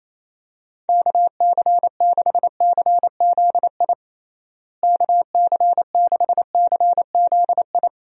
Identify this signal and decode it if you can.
Morse code, heard as a keyed tone, about 28 wpm, 700 Hz. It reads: KC6CZI KC6CZI
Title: USA Call Signs - 6 Characters - Repeated 2x - 28wpm